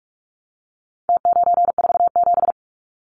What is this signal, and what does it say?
Morse code, heard as a keyed tone, about 44 wpm, 700 Hz. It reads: T947
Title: Numbers - Contest Style - 4-Digit 45wpm